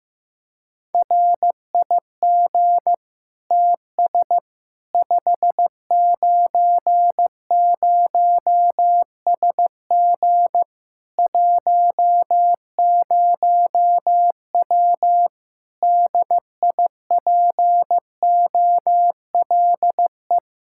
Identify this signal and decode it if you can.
Morse code, heard as a keyed tone, about 15 wpm, 700 Hz. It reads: RIG TS 590SG 10W DIPOLE
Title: K1USN SST QSO Practice 15wpm